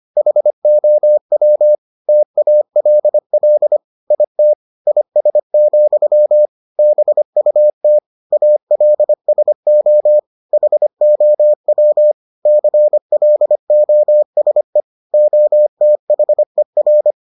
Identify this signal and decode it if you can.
Morse code, heard as a keyed tone, about 25 wpm, 600 Hz. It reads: HOW TALL IT IS, BUT ALSO HOW CLOSE OTHER